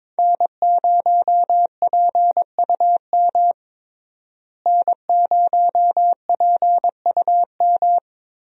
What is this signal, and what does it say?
Morse code, heard as a keyed tone, about 22 wpm, 700 Hz. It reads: N0PUM N0PUM
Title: USA Call Signs - 5 Characters - Repeated 2x - 22wpm